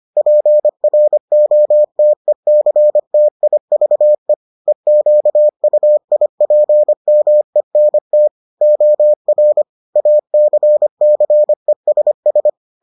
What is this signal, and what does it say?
Morse code, heard as a keyed tone, about 25 wpm, 600 Hz. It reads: PROTECTIVE EQUIPMENT OR ACCESS